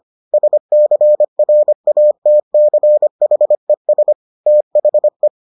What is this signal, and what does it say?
Morse code, heard as a keyed tone, about 25 wpm, 600 Hz. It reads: SCRATCHES THE